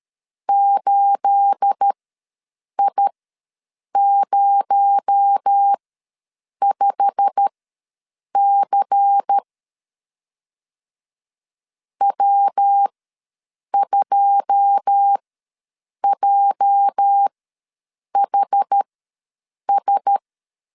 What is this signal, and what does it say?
Morse code, heard as a keyed tone, about 13 wpm, spaced as for 7 wpm, 800 Hz. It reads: 8I05C W2JHS
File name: LEZIONI / 9 28 ws 12wpm 3354